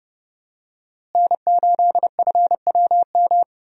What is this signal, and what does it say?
Morse code, heard as a keyed tone, about 30 wpm, 700 Hz. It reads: N8FWM
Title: USA Call Signs - 5 Characters - Repeated 2x - 30wpm